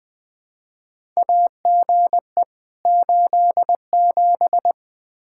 Morse code, heard as a keyed tone, about 20 wpm, 700 Hz. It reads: AGE 87
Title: K1USN SST QSO Practice 20wpm